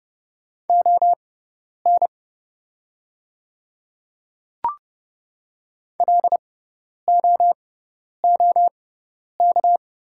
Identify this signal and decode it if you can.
Morse code, heard as a keyed tone, about 30 wpm, spaced as for 10 wpm, 700 Hz. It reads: ON LOOK
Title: 6x Character Spacing - Top 100 Words - 30wpm